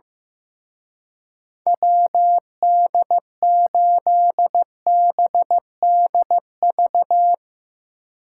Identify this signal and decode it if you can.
Morse code, heard as a keyed tone, about 16 wpm, 700 Hz. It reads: WD8BDV
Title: Speed Racing - USA 2x3 Call Signs 15wpm